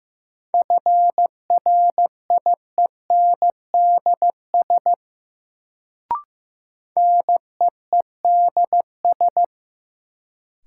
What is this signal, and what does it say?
Morse code, heard as a keyed tone, about 15 wpm, 700 Hz. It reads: FRIENDS NEEDS
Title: Sentences from Top 1600 Words - New Word Drill Alt - 15wpm